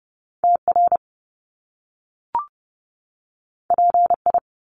Morse code, heard as a keyed tone, about 30 wpm, 700 Hz. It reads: TR PI
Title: Top Two Letter Combinations in Words 30wpm